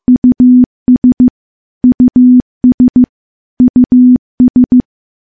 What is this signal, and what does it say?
Morse code, heard as a keyed tone, about 15 wpm, 250 Hz. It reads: US US US
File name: morse-training-15WPM